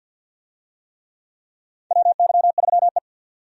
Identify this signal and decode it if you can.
Morse code, heard as a keyed tone, about 51 wpm, 700 Hz. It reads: WY3E
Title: USA Call Signs - 4 Characters - 50wpm